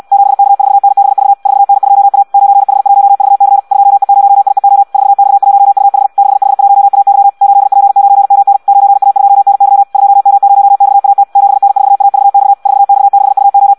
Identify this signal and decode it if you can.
Morse code, heard as a keyed tone, about 74 wpm, 800 Hz. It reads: 7K4NBV 4N2A 9H1VC 2E0IEO 4X1UH 6H1AY 8P9MN 9A2NY 2M0XAT 7N4AVX 4X6UO